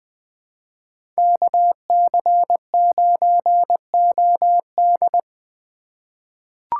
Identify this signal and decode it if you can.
Morse code, heard as a keyed tone, about 20 wpm, 700 Hz. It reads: KC9OD
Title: USA Call Signs - 5 Characters - 3x Character Spacing - 20wpm